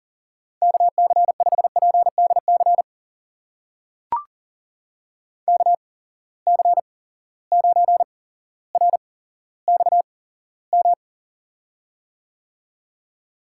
Morse code, heard as a keyed tone, about 40 wpm, spaced as for 17 wpm, 700 Hz. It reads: KC5PDC KC9RXM
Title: USA Call Signs - 6 Characters - 8x Character Spacing - 40wpm